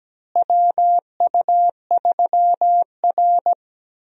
Morse code, heard as a keyed tone, about 17 wpm, 700 Hz. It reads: WU3R
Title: USA Call Signs - 4 Characters - Repeated 2x - 17wpm